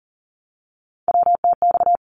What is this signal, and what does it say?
Morse code, heard as a keyed tone, about 42 wpm, 700 Hz. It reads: WTX
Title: ARRL Sections 40wpm